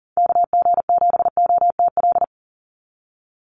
Morse code, heard as a keyed tone, about 40 wpm, 700 Hz. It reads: KG7OTL
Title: USA Call Signs 40wpm